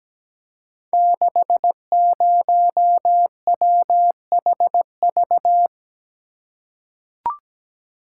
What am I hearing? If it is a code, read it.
Morse code, heard as a keyed tone, about 17 wpm, 700 Hz. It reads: 60WHV
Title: Code Groups - 5 Letter:Numbers - 17wpm